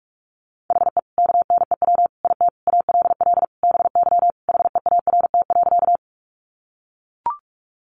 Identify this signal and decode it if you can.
Morse code, heard as a keyed tone, about 44 wpm, 700 Hz. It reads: HE KNEW IT ALL BY HEART.
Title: Sentences from Top 1200 Words 45wpm